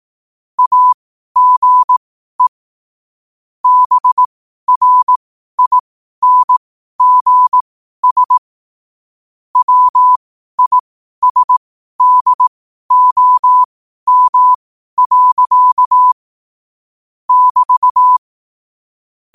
Morse code, heard as a keyed tone, about 18 wpm, spaced as for 12 wpm, 1000 Hz. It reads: AGE BRINGS WISDOM. <BT>
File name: qotdfiles / 12wpm